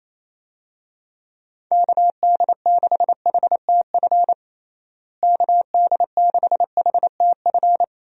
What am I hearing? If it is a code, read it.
Morse code, heard as a keyed tone, about 28 wpm, 700 Hz. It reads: KD6HTF KD6HTF
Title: USA Call Signs - 6 Characters - Repeated 2x - 28wpm